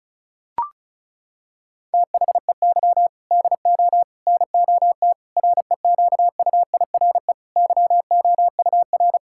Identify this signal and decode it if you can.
Morse code, heard as a keyed tone, about 36 wpm, 700 Hz. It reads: THEY DO NOT REQUIRE YOUR
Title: Sentences from Top 1000 Words - Review Part 6 - 35wpm